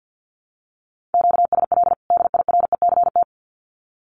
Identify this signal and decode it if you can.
Morse code, heard as a keyed tone, about 50 wpm, 700 Hz. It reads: QSL DIRECT
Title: Common QSO Phrases 50wpm